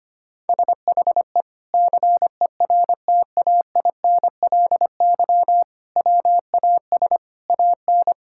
Morse code, heard as a keyed tone, about 25 wpm, 700 Hz. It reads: SHE CERTAINLY WAS AN
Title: Sentences from Top 1000 Words - Review Part 6 - 25wpm